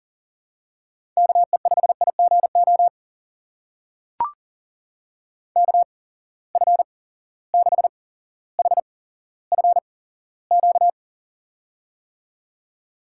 Morse code, heard as a keyed tone, about 39 wpm, spaced as for 17 wpm, 700 Hz. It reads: KE5IGO KF6HFQ
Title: USA Call Signs - 6 Characters - 8x Character Spacing - 40wpm